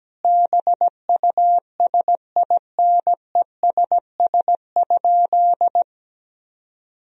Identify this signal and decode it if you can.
Morse code, heard as a keyed tone, about 17 wpm, 700 Hz. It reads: BUSINESS?
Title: Sentences from Top 1000 Words - Review Part 4 - 17wpm